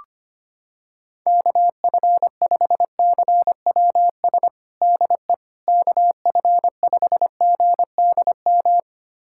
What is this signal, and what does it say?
Morse code, heard as a keyed tone, about 25 wpm, 700 Hz. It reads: KF5CWS DE KF5GDM
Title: USA Callsign DE Callsign - 6 Characters - 25wpm